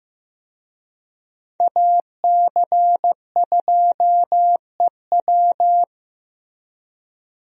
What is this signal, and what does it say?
Morse code, heard as a keyed tone, about 16 wpm, 700 Hz. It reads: AC2EW
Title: USA Call Signs - 5 Characters - Repeated 3x - 15wpm